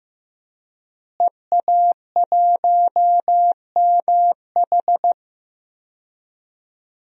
Morse code, heard as a keyed tone, about 15 wpm, 700 Hz. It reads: EA1MH
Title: DXCC-100 Call Signs - 5 Characters - Repeated 3x - 15wpm